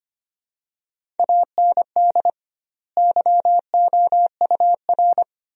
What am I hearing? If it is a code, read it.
Morse code, heard as a keyed tone, about 25 wpm, 700 Hz. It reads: AND YOUR
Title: 2x Word-Spacing - Sentences from Top 300 Words - 25wpm